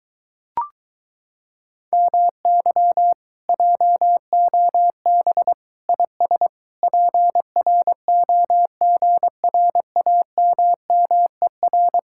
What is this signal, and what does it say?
Morse code, heard as a keyed tone, about 23 wpm, 700 Hz. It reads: MY JOB IS PROGRAMMER